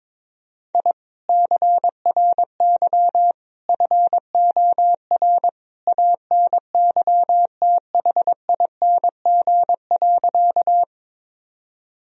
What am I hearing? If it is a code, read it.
Morse code, heard as a keyed tone, about 22 wpm, 700 Hz. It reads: I CRY FOR ANYTHING.
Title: Sentences from Top 1000 Words - Review Part 2 - 22wpm